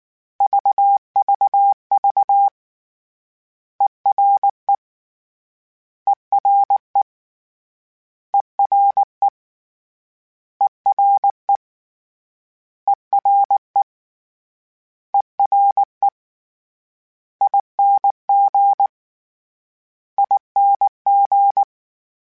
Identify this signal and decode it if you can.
Morse code, heard as a keyed tone, about 19 wpm, 800 Hz. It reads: VVV ERE ERE ERE ERE ERE ERE ING ING
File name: Special INGs / Special INGs-mp3s / Ere ing @18